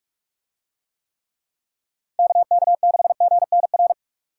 Morse code, heard as a keyed tone, about 44 wpm, 700 Hz. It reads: KK6GNR